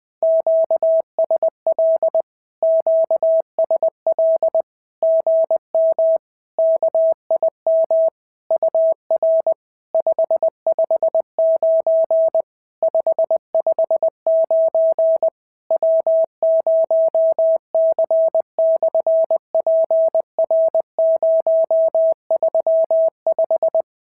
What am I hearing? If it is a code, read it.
Morse code, heard as a keyed tone, about 20 wpm, 650 Hz. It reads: QSL QSL GM KIM UR 559 559 W0C/PR035
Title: SOTA QSO Practice 20wpm